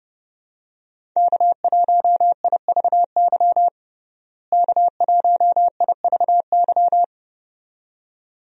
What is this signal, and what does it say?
Morse code, heard as a keyed tone, about 30 wpm, 700 Hz. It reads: K1IVY K1IVY
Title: USA Call Signs - 5 Characters - Repeated 2x - 30wpm